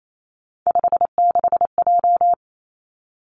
Morse code, heard as a keyed tone, about 28 wpm, 700 Hz. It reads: 56J
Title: Three Challenging Letter-Number Combinations 28wpm